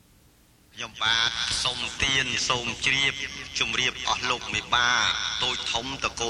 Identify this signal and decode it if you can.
Emergency Alert System (SAME) tones (header bursts), heard as harsh buzzing bursts, with speech. It is heard from 1.0 to 5.3 s.